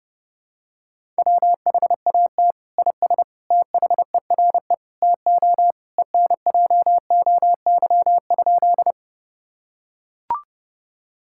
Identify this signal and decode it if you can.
Morse code, heard as a keyed tone, about 30 wpm, 700 Hz. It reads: WHAT IS THERE TO ENJOY?
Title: Sentences from Top 1000 Words - Review Part 5 - 30wpm